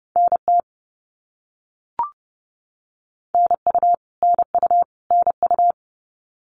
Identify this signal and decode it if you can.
Morse code, heard as a keyed tone, about 30 wpm, 700 Hz. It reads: NT NU NU NU
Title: Canadian Provinces and Territories - Repeated 3x - 30wpm